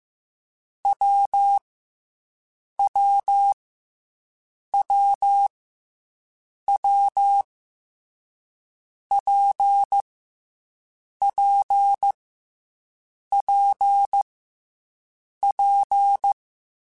Morse code, heard as a keyed tone, about 15 wpm, 800 Hz. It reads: W W W W P P P P